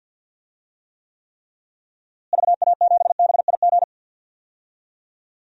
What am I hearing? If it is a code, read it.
Morse code, heard as a keyed tone, about 52 wpm, 700 Hz. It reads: VA7BIG
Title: DXCC-100 Call Signs - 6 Characters - Repeated 3x - 50wpm